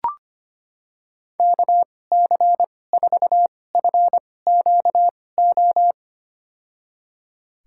Morse code, heard as a keyed tone, about 25 wpm, spaced as for 18 wpm, 700 Hz. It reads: KC4FQO
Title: USA Call Signs - 6 Characters - 2x Character Spacing - 25wpm